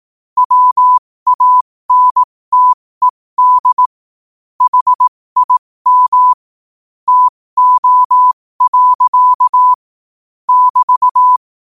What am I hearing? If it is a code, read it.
Morse code, heard as a keyed tone, about 18 wpm, spaced as for 15 wpm, 1000 Hz. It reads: WANTED HIM TO. <BT>
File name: qotdfiles / 15wpm